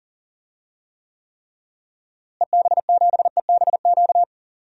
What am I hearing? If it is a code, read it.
Morse code, heard as a keyed tone, about 40 wpm, 700 Hz. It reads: EB7EBQ
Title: DXCC-100 Call Signs - 6 Characters - Repeated 3x - 40wpm